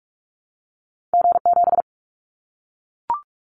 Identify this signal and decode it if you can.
Morse code, heard as a keyed tone, about 44 wpm, 700 Hz. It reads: G7
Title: Callsign Prefixes for Top 25 DXCC - Repeated 3x - 45wpm